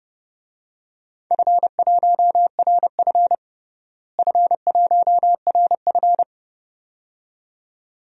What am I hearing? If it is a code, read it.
Morse code, heard as a keyed tone, about 30 wpm, 700 Hz. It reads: F1RF F1RF
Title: DXCC-100 Call Signs - 4 Characters - Repeated 2x - 30wpm